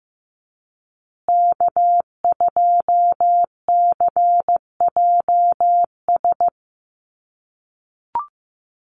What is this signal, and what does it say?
Morse code, heard as a keyed tone, about 15 wpm, 700 Hz. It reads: K2CJS
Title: USA Call Signs Repeated 3x 15wpm